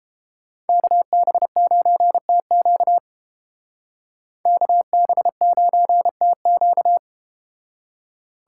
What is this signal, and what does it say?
Morse code, heard as a keyed tone, about 30 wpm, 700 Hz. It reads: KB9TQ KB9TQ